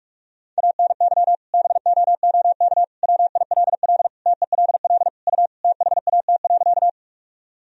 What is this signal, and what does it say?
Morse code, heard as a keyed tone, about 46 wpm, 700 Hz. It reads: ANY BOOK WILL TELL U THAT.
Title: Sentences from Top 300 Words and Abbreviations 45wpm